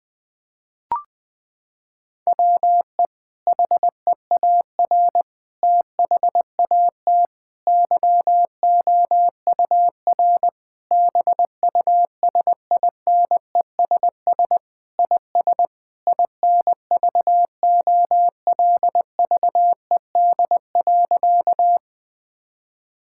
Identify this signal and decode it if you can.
Morse code, heard as a keyed tone, about 20 wpm, 700 Hz. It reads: WE HEAR THAT YOUR BUSINESS IS INVOLVED.
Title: Sentences from Top 1000 Words - Review Part 5 - 20wpm